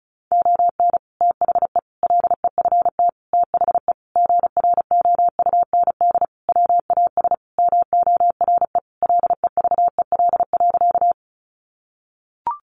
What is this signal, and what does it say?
Morse code, heard as a keyed tone, about 34 wpm, 700 Hz. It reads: ON THE LEFT THE GROUND WAS MORE LEVEL.
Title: Sentences from Top 700 Words 35wpm